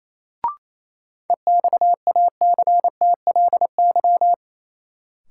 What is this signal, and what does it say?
Morse code, heard as a keyed tone, about 28 wpm, 700 Hz. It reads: EXACTLY
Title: Sentences from Top 900 Words - New Word Drill - 28wpm